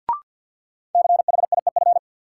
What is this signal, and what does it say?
Morse code, heard as a keyed tone, about 48 wpm, 700 Hz. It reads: CHIEF
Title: Sets of 5 Words Encore - New Word Drill - 50wpm